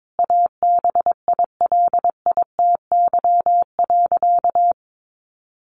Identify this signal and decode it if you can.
Morse code, heard as a keyed tone, about 22 wpm, 700 Hz. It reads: ABILITY.
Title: Sentences from Top 1200 Words 22wpm